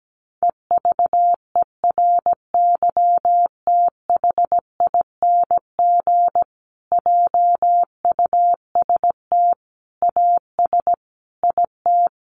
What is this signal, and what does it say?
Morse code, heard as a keyed tone, about 17 wpm, 700 Hz. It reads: EVERYTHING JUST AS IT
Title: Sentences from Top 700 Words 17wpm